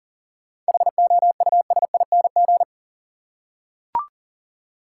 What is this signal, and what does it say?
Morse code, heard as a keyed tone, about 40 wpm, 700 Hz. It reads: HOUSING
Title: Seven Letter Words - 40wpm